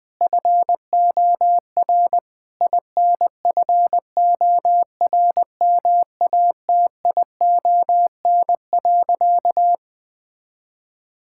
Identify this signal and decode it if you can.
Morse code, heard as a keyed tone, about 21 wpm, 700 Hz. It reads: FOR INFORMATION.
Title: Sentences from Top 1000 Words - Review Part 6 - 20wpm